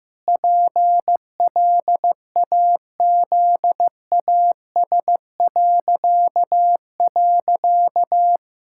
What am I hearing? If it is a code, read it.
Morse code, heard as a keyed tone, about 15 wpm, 700 Hz. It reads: PLAZAS..
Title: Pangram Challenge v2 - 15wpm